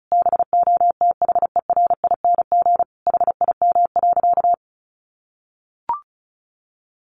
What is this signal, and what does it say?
Morse code, heard as a keyed tone, about 34 wpm, 700 Hz. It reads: BOTHERING HIM.
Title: Additional Easy Phrases 35wpm